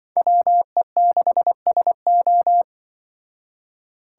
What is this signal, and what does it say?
Morse code, heard as a keyed tone, about 24 wpm, 700 Hz. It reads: WE6SO
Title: Speed Racing - USA Call Signs 20wpm